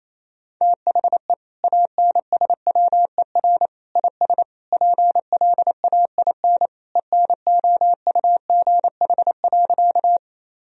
Morse code, heard as a keyed tone, about 28 wpm, 700 Hz. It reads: THE ANSWER IS PLAIN ENOUGH.
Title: Sentences from Top 1000 Words - Review Part 2 - 28wpm